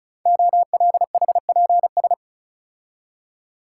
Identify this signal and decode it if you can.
Morse code, heard as a keyed tone, about 36 wpm, 690 Hz. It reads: OLHPS
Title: Code Groups - 5 Letters - 35wpm